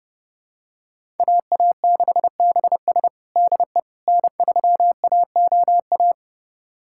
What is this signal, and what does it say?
Morse code, heard as a keyed tone, about 32 wpm, 700 Hz. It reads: AA6BS DE N3AOA
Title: USA Callsign DE Callsign - 5 Characters - 30wpm